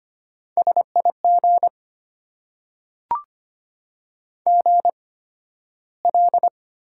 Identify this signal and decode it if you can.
Morse code, heard as a keyed tone, about 25 wpm, spaced as for 13 wpm, 700 Hz. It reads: SIG GL
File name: Top 90 QSO Elements - 8x Character Spacing - 25wpm